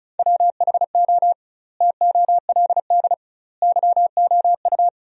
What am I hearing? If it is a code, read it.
Morse code, heard as a keyed tone, about 36 wpm, 700 Hz. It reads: WHO TOLD YOU
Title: 2x Word-Spacing - Sentences from Top 400 Words - 35wpm